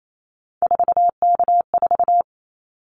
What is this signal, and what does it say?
Morse code, heard as a keyed tone, about 28 wpm, 700 Hz. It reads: 4K4
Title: Callsign Prefixes for Top 100 DXCC - 28wpm